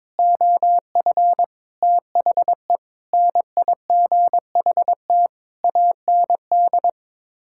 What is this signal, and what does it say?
Morse code, heard as a keyed tone, about 21 wpm, 700 Hz. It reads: OF THE NIGHT AND
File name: practice / 22wpm / Sets of 7 Words - 22wpm